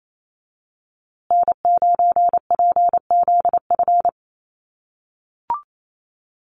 Morse code, heard as a keyed tone, about 28 wpm, 700 Hz. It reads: N9PZF
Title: USA Call Signs Repeated 3x 28wpm